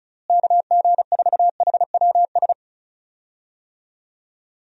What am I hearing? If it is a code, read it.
Morse code, heard as a keyed tone, about 34 wpm, 700 Hz. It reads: KG4HWS
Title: USA Call Signs - 6 Characters - Repeated 2x - 35wpm